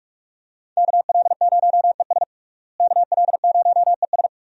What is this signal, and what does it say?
Morse code, heard as a keyed tone, about 44 wpm, 700 Hz. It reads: KL0ES KL0ES
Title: USA Call Signs - 5 Characters - Repeated 2x - 45wpm